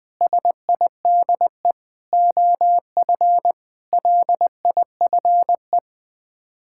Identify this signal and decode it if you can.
Morse code, heard as a keyed tone, about 20 wpm, 700 Hz. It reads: SIDE OF LIFE
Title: Sets of 8 Words - 20wpm